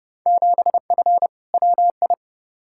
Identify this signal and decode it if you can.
Morse code, heard as a keyed tone, about 31 wpm, 700 Hz. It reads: 7F WI
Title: Field Day Exchanges Encore - 30wpm